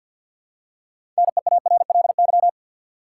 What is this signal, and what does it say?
Morse code, heard as a keyed tone, about 53 wpm, 700 Hz. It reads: NEARLY
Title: Six Letter Words - 50wpm